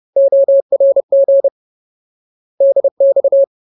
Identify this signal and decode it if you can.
Morse code, heard as a keyed tone, about 30 wpm, 550 Hz. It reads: ORG DX
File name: Ultimate Word List / 30WPM Ultimate